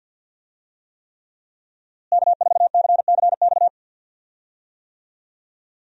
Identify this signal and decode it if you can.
Morse code, heard as a keyed tone, about 51 wpm, 700 Hz. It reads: K4CCX
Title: USA Call Signs - 5 Characters - Repeated 3x - 50wpm